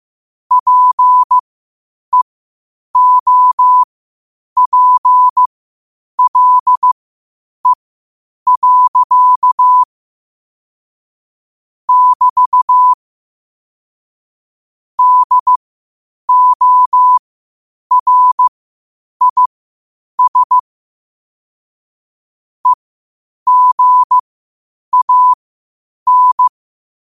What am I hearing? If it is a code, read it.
Morse code, heard as a keyed tone, about 15 wpm, spaced as for 8 wpm, 1000 Hz. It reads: PEOPLE. <BT> DORIS EGAN
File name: qotdfiles / 8wpm